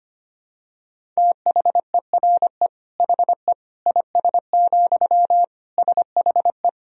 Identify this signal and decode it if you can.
Morse code, heard as a keyed tone, about 25 wpm, 700 Hz. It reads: THERE HE IS, SHE
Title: Sentences from Top 100 Words and Abbreviations 25wpm